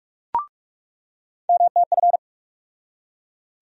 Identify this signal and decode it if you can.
Morse code, heard as a keyed tone, about 42 wpm, 700 Hz. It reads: MTF
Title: USA Call Sign Postfixes - 3 Letters 45wpm